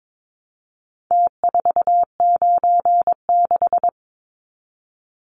Morse code, heard as a keyed tone, about 22 wpm, 700 Hz. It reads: T496
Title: Numbers - Contest Style - 4-Digit 22wpm